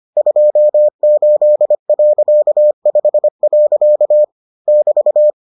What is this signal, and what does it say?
Morse code, heard as a keyed tone, about 25 wpm, 600 Hz. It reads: 28.5. <BT>